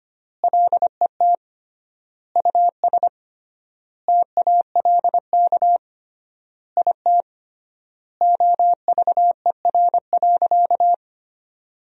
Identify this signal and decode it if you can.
Morse code, heard as a keyed tone, about 25 wpm, 700 Hz. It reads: LET US TALK IT OVER.
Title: 3x Word-Spacing - Sentences from Top 500 Words - 25wpm